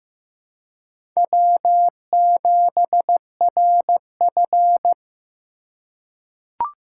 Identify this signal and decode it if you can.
Morse code, heard as a keyed tone, about 15 wpm, 700 Hz. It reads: W7RF
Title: USA Call Signs - 4 Characters - 3x Character Spacing - 15wpm